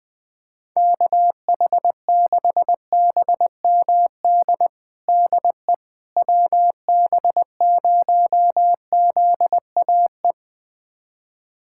Morse code, heard as a keyed tone, about 21 wpm, 700 Hz. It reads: KH6BMD DE WB0ZAE
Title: USA Callsign DE Callsign - 6 Characters - 20wpm